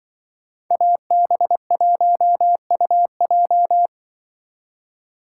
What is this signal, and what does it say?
Morse code, heard as a keyed tone, about 23 wpm, 700 Hz. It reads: AB1UJ
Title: Speed Racing - USA Call Signs 20wpm